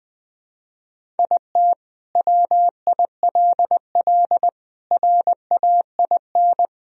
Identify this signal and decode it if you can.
Morse code, heard as a keyed tone, about 20 wpm, 700 Hz. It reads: IT WILL RAIN
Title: Sentences from Top 1000 Words - Review Part 1 - 20wpm